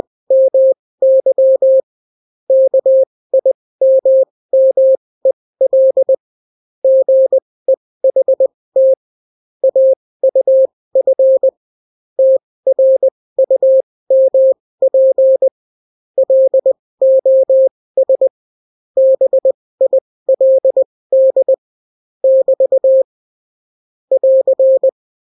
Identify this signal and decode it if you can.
Morse code, heard as a keyed tone, about 20 wpm, spaced as for 16 wpm, 530 Hz. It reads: MY KIMMEL GEHT AUF TRUMP LOS BILD <BT> <AR>